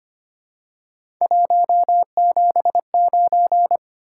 Morse code, heard as a keyed tone, about 25 wpm, 700 Hz. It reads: 179
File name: Three Letter-Number Combinations - 25wpm